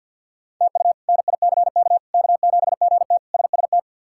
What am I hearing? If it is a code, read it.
Morse code, heard as a keyed tone, about 50 wpm, 700 Hz. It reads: TU NICK K7GT SST
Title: K1USN SST QSO Practice - 50wpm